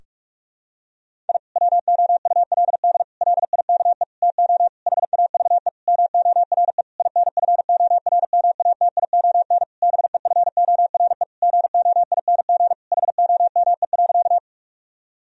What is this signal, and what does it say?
Morse code, heard as a keyed tone, about 46 wpm, 700 Hz. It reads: I WOULD LIKE TO HAVE MORE INFORMATION BEFORE GOING HOME.
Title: Sentences from Top 1000 Words - Review Part 3 - 45wpm